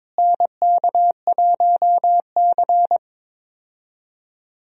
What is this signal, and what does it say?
Morse code, heard as a keyed tone, about 22 wpm, 700 Hz. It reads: NK1C
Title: USA Call Signs - 4 Characters - Repeated 2x - 22wpm